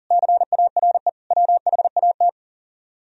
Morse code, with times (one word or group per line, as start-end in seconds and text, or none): 0.1-1.1 CARE
1.3-2.3 WHAT